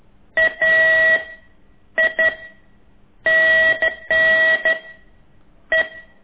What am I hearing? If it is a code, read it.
Morse code, heard as a keyed tone, about 11 wpm, 1950 Hz. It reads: A I N N E
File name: ik1weg2 / CORSI CW